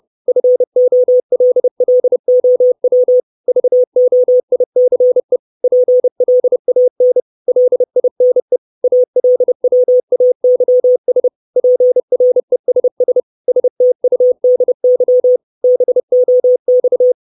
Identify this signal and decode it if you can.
Morse code, heard as a keyed tone, about 30 wpm, 500 Hz. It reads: FOLLOW VOICE PLAN LINE ALWAYS PRESS STUDY BOX
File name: 632 Words 30WPM